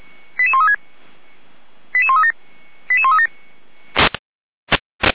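Five-tone selective call (ZVEI), heard as quick runs of five tones, tones of about 70 ms. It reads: 88136 88137 88137
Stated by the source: ZVEI five-tone call 88402 received on SDR, from Feuerwehr Orlamünde.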